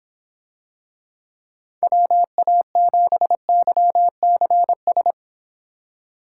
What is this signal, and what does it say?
Morse code, heard as a keyed tone, about 25 wpm, 700 Hz. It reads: WA7YCS